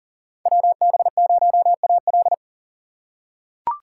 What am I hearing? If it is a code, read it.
Morse code, heard as a keyed tone, about 39 wpm, 700 Hz. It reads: WB0AL
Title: USA Call Signs - 5 Characters - 6x Character Spacing - 40wpm